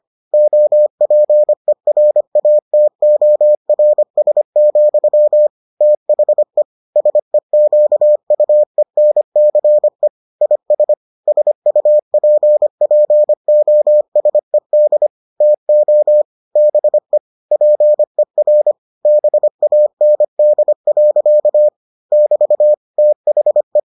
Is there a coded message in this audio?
Yes: Morse code from 0.3 to 23.8 s, heard as a keyed tone, about 25 wpm, 600 Hz. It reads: OPERATORS, THE SEQUENCE IS SUPPOSED TO BE PER BAND. <BT> THE